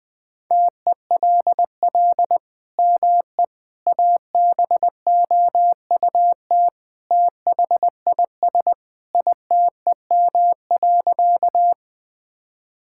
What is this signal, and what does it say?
Morse code, heard as a keyed tone, about 20 wpm, 700 Hz. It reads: TELL ME ABOUT THIS ITEM.
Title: Sentences from Top 1000 Words - Review Part 6 - 20wpm